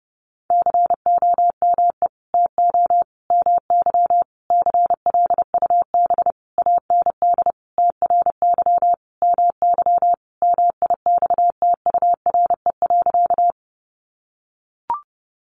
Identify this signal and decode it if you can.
Morse code, heard as a keyed tone, about 30 wpm, 700 Hz. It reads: COME TO MY CLUB AND TRY MY MIXTURE.
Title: Sentences from Top 1500 Words 30wpm